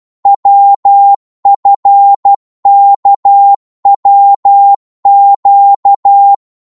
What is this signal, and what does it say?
Morse code, heard as a keyed tone, about 12 wpm, 800 Hz. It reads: WFKWQ
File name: cwmp3 / ltr-12wpm